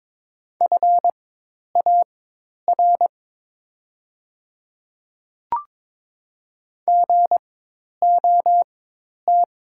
Morse code, heard as a keyed tone, about 22 wpm, spaced as for 10 wpm, 690 Hz. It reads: FAR GOT
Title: 4x Character Spacing - Top 300 Words - 22wpm